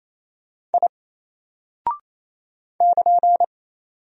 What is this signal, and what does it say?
Morse code, heard as a keyed tone, about 27 wpm, 700 Hz. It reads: I <KN>
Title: Single Character (Letter:Number:Punctuation:Prosign) Alt 28wpm